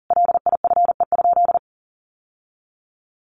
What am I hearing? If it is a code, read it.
Morse code, heard as a keyed tone, about 37 wpm, 700 Hz. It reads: LIFE?
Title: Sentences from Top 1200 Words 40wpm